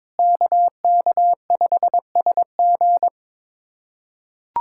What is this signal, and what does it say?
Morse code, heard as a keyed tone, about 22 wpm, 700 Hz. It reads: KK5SG
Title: USA Callsign DE Callsign - 5 Characters - 22wpm